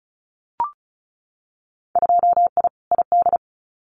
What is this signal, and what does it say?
Morse code, heard as a keyed tone, about 34 wpm, 700 Hz. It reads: 2I ID
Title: Winter Field Day Exchanges Encore 35wpm